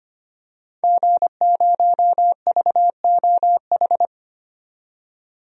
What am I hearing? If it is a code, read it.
Morse code, heard as a keyed tone, about 25 wpm, 700 Hz. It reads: G0VOH
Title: DXCC-100 Call Signs - 5 Characters - 25wpm